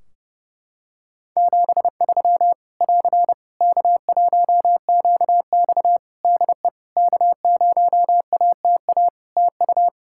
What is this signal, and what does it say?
Morse code, heard as a keyed tone, about 30 wpm, 700 Hz. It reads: 73 <AR> K1QX DE K0ATA TU
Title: Practice QSO - Exchange 4 - Ending - 30wpm